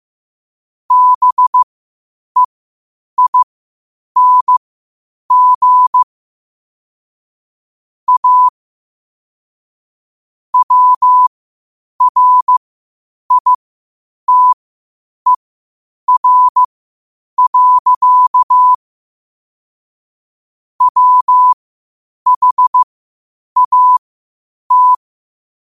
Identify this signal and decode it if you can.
Morse code, heard as a keyed tone, about 15 wpm, spaced as for 8 wpm, 1000 Hz. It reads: BEING A WRITER. WHAT
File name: qotdfiles / 8wpm